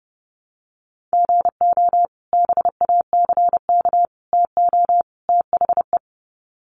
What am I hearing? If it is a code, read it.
Morse code, heard as a keyed tone, about 30 wpm, 700 Hz. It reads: GO BACK TO THE
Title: Sets of 4 Words 30wpm